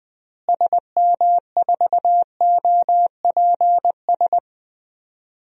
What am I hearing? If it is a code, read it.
Morse code, heard as a keyed tone, about 20 wpm, 700 Hz. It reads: SM4OPS
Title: DXCC-100 Call Signs - 6 Characters - Repeated 3x - 20wpm